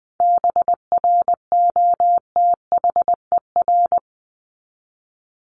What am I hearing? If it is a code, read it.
Morse code, heard as a keyed tone, about 20 wpm, 700 Hz. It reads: BROTHER
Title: Top 1751-2000 Words 20wpm